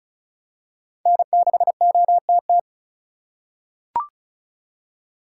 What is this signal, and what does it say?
Morse code, heard as a keyed tone, about 34 wpm, 700 Hz. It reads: N6OTT
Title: USA Call Signs - 5 Characters - 3x Character Spacing - 35wpm